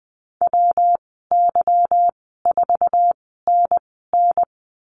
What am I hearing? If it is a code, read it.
Morse code, heard as a keyed tone, about 20 wpm, spaced as for 15 wpm, 700 Hz. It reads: WY4NN
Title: 200 Lesson 32g - Letter k - Call Signs 20wpm